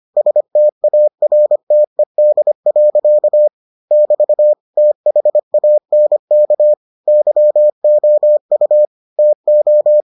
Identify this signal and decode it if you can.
Morse code, heard as a keyed tone, about 25 wpm, 600 Hz. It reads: STARTED. <BT> THANK YOU TO